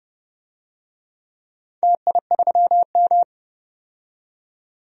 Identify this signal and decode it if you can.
Morse code, heard as a keyed tone, about 30 wpm, 700 Hz. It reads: TI3M
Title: DXCC-100 Call Signs - 4 Characters - 30wpm